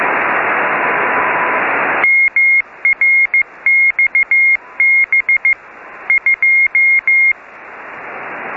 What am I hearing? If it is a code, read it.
Morse code, heard as a keyed tone, about 15 wpm, 2150 Hz. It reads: MRXB 2